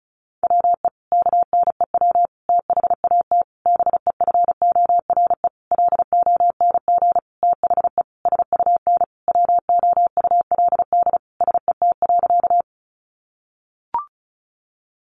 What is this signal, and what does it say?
Morse code, heard as a keyed tone, about 34 wpm, 700 Hz. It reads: WE KNEW THAT BEFORE LONG THE SUN WOULD SET.
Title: Sentences from Top 400 Words 35wpm